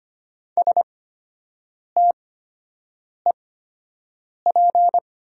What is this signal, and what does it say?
Morse code, heard as a keyed tone, about 25 wpm, spaced as for 7 wpm, 700 Hz. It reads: STEP
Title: 8x Character Spacing - Top 500 Words - 25wpm